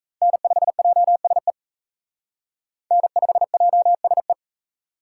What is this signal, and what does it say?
Morse code, heard as a keyed tone, about 35 wpm, 700 Hz. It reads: N5JSE N5JSE
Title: Speed Racing - USA 1x3 Call Signs 28wpm